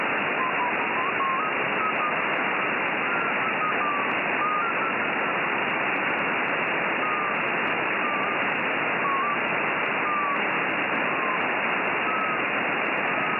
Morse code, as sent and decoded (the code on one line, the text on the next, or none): none